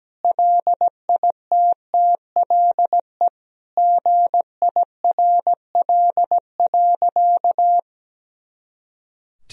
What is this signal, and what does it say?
Morse code, heard as a keyed tone, about 17 wpm, 700 Hz. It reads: LITTLE GIRL.
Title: Sentences from Top 400 Words and Abbreviations 17wpm